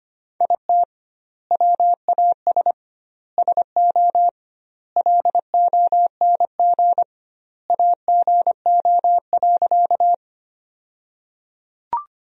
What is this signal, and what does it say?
Morse code, heard as a keyed tone, about 25 wpm, 700 Hz. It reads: IT WAS SO LONG AGO.
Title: 2x Word-Spacing - Sentences from Top 500 Words - 25wpm